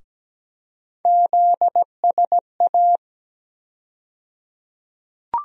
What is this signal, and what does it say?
Morse code, heard as a keyed tone, about 17 wpm, 700 Hz. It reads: ZSA
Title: USA Call Sign Postfixes - 3 Letters 17wpm